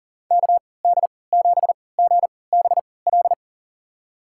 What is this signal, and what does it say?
Morse code, heard as a keyed tone, about 40 wpm, spaced as for 23 wpm, 700 Hz. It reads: KD7GBL
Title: USA Call Signs - 6 Characters - 3x Character Spacing - 40wpm